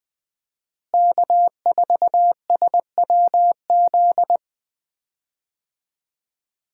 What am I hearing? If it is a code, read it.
Morse code, heard as a keyed tone, about 21 wpm, 700 Hz. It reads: K4SWZ
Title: USA Call Signs - 5 Characters - Repeated 2x - 20wpm